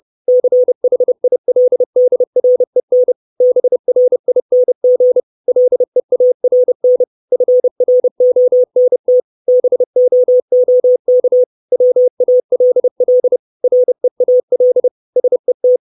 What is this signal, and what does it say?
Morse code, heard as a keyed tone, about 30 wpm, 500 Hz. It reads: CHILDREN BRING LEARN FRONT BOOK WALL REAL SET
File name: 632 Words 30WPM